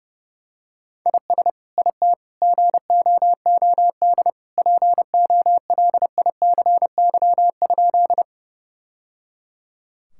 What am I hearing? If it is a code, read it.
Morse code, heard as a keyed tone, about 30 wpm, 700 Hz. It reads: IS IT GOOD POLICY?
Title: Sentences from Top 1000 Words - Review Part 5 - 30wpm